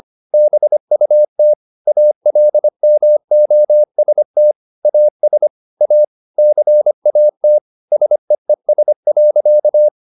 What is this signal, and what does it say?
Morse code, heard as a keyed tone, about 25 wpm, 600 Hz. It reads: BUT ALMOST AS A CAT SEES.
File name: the old man and the sea part 1 25wpm